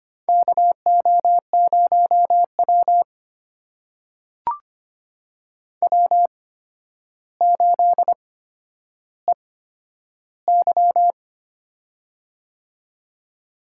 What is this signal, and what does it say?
Morse code, heard as a keyed tone, about 25 wpm, spaced as for 11 wpm, 700 Hz. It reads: KO0W W8EY
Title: USA Call Signs - 4 Characters - 8x Character Spacing - 25wpm